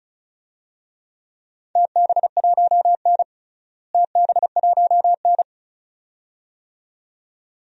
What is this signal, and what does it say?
Morse code, heard as a keyed tone, about 34 wpm, 700 Hz. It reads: TB1N TB1N
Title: DXCC-100 Call Signs - 4 Characters - Repeated 2x - 35wpm